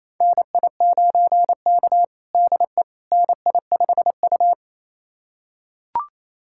Morse code, heard as a keyed tone, about 28 wpm, 700 Hz. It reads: NI9K DE NI5U
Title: USA Callsign DE Callsign - 4 Characters - 28wpm